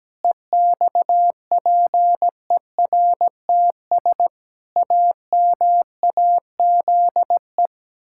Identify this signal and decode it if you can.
Morse code, heard as a keyed tone, about 17 wpm, 700 Hz. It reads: EXPERTS AMAZE
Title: Pangram Challenge v2 - 17wpm